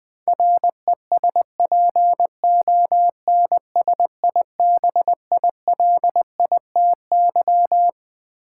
Morse code, heard as a keyed tone, about 20 wpm, 700 Hz. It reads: RESPONSIBILITY
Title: Sentences from Top 1200 Words - New Word Drill Alt - 20wpm